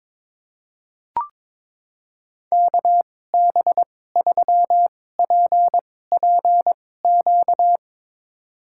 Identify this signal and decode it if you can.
Morse code, heard as a keyed tone, about 21 wpm, spaced as for 16 wpm, 700 Hz. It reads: KB3PPQ
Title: USA Call Signs - 6 Characters - 2x Character Spacing - 22wpm